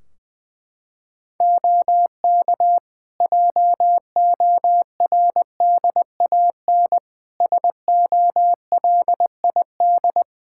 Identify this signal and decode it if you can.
Morse code, heard as a keyed tone, about 20 wpm, 700 Hz. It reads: OK JORDAN SOLID
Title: Practice QSO - Exchange 3 - More Info - 20wpm